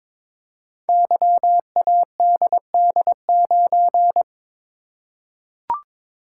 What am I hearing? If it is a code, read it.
Morse code, heard as a keyed tone, about 22 wpm, 700 Hz. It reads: YADD9
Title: Code Groups - 5 Letter:Numbers - 22wpm